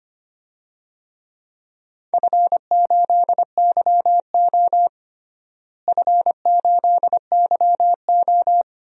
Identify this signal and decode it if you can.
Morse code, heard as a keyed tone, about 25 wpm, 700 Hz. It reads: F8YO F8YO
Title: DXCC-100 Call Signs - 4 Characters - Repeated 3x - 25wpm